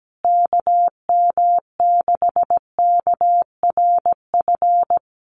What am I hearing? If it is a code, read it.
Morse code, heard as a keyed tone, about 17 wpm, 700 Hz. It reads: KM6KRF
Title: USA Call Signs Repeated 3x 17wpm